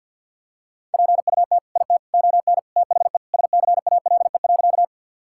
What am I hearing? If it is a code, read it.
Morse code, heard as a keyed tone, about 51 wpm, 700 Hz. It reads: PUT IT ON THE SCALE.
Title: Sentences from Top 1000 Words - Review Part 5 - 50wpm